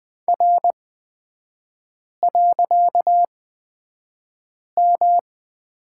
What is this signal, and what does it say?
Morse code, heard as a keyed tone, about 20 wpm, spaced as for 5 wpm, 700 Hz. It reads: R.M
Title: 255 Lesson 41e - Period - Three Characters 20wpm